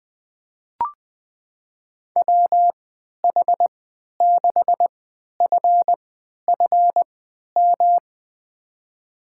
Morse code, heard as a keyed tone, about 20 wpm, spaced as for 11 wpm, 700 Hz. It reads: WH6FFM
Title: USA Call Signs - 6 Characters - 3x Character Spacing - 20wpm